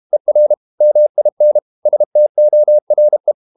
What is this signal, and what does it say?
Morse code, heard as a keyed tone, about 32 wpm, 600 Hz. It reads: ER MIN STORE
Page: Ferie 32 wpm